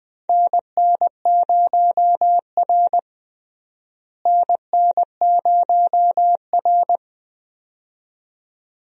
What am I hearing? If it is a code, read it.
Morse code, heard as a keyed tone, about 20 wpm, 700 Hz. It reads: NN0R NN0R
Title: USA Call Signs - 4 Characters - Repeated 2x - 20wpm